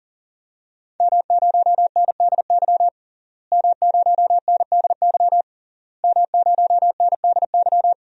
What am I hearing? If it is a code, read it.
Morse code, heard as a keyed tone, about 40 wpm, 700 Hz. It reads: M0NDY M0NDY M0NDY
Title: DXCC-100 Call Signs - 5 Characters - Repeated 3x - 40wpm